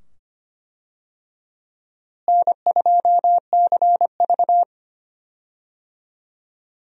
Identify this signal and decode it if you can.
Morse code, heard as a keyed tone, about 25 wpm, 700 Hz. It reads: N2CV
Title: USA Call Signs - 4 Characters - Repeated 2x - 25wpm